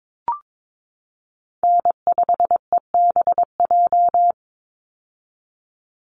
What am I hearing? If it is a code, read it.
Morse code, heard as a keyed tone, about 22 wpm, 700 Hz. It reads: N5EBJ
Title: USA Call Signs 22wpm